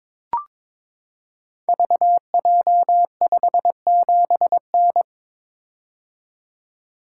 Morse code, heard as a keyed tone, about 23 wpm, 700 Hz. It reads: VJ57N
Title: Code Groups - 5 Letter:Numbers - 22wpm